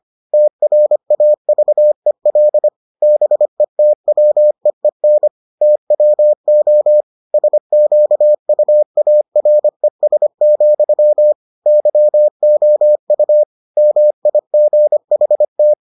Morse code, heard as a keyed tone, about 25 wpm, 600 Hz. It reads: TRAVEL BETWEEN TWO SQUARES, YOU MIGHT